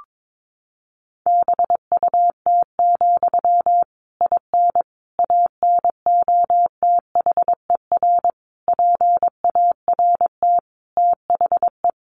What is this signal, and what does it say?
Morse code, heard as a keyed tone, about 22 wpm, 700 Hz. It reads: BUT, IN ANOTHER PART THE
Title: Sentences from Top 1500 Words 22wpm